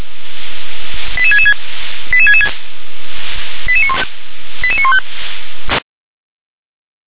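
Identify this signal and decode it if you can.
Five-tone selective call (ZVEI), heard as quick runs of five tones, tones of about 70 ms. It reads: 88666 88666 88915
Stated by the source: ZVEI five-tone selective call received by SDR.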